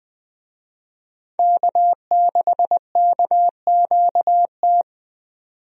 Morse code, heard as a keyed tone, about 21 wpm, 700 Hz. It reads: K6KQT
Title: USA Call Signs - 5 Characters - 20wpm